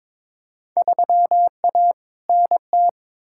Morse code, heard as a keyed tone, about 21 wpm, 700 Hz. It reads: 3A NT
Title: Field Day Exchanges Encore - 22wpm